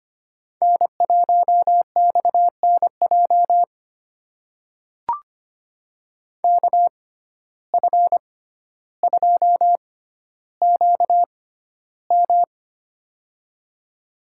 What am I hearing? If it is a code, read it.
Morse code, heard as a keyed tone, about 25 wpm, spaced as for 13 wpm, 700 Hz. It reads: N1XNJ KF2QM
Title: USA Call Signs - 5 Characters - 6x Character Spacing - 25wpm